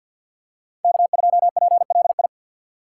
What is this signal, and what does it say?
Morse code, heard as a keyed tone, about 52 wpm, 700 Hz. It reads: K2PLI
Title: USA Call Signs - 5 Characters - 6x Character Spacing - 50wpm